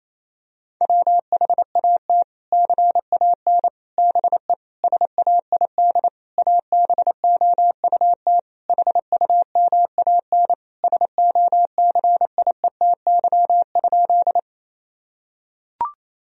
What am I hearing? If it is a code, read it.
Morse code, heard as a keyed tone, about 28 wpm, 700 Hz. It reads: WHAT CAN BE SAID ABOUT HUMAN SOCIETY?
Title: Sentences from Top 1000 Words - Review Part 4 - 28wpm